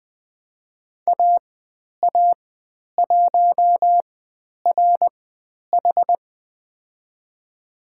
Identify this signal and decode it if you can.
Morse code, heard as a keyed tone, about 20 wpm, spaced as for 10 wpm, 700 Hz. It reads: AA1RH
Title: 102 Lesson 17g - Letter u - Call Signs 20wpm